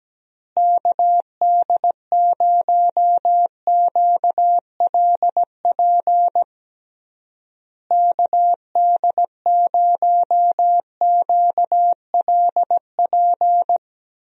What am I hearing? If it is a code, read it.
Morse code, heard as a keyed tone, about 17 wpm, 700 Hz. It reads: KD0QLP KD0QLP
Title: USA Call Signs - 6 Characters - Repeated 3x - 17wpm